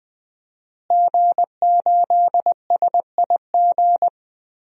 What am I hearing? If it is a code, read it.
Morse code, heard as a keyed tone, about 20 wpm, 700 Hz. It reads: G8SIG
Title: DXCC-100 Call Signs - 5 Characters - Repeated 3x - 20wpm